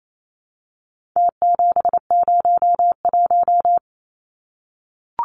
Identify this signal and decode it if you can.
Morse code, heard as a keyed tone, about 28 wpm, 700 Hz. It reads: T701
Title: Numbers - Contest Style - 4-Digit 28wpm